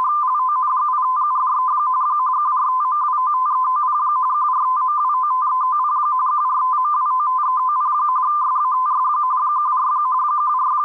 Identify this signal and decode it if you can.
Two-tone teleprinter signal, about 170 Hz shift.